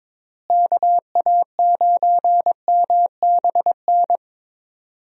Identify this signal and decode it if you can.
Morse code, heard as a keyed tone, about 22 wpm, 700 Hz. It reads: KA9MBN
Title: USA Call Signs - 6 Characters - Repeated 2x - 22wpm